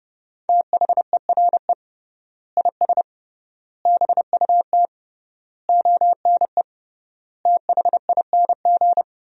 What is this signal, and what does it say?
Morse code, heard as a keyed tone, about 32 wpm, 700 Hz. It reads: THERE IS BUT ONE THING
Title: 3x Word-Spacing - Sentences from Top 100 Words - 30wpm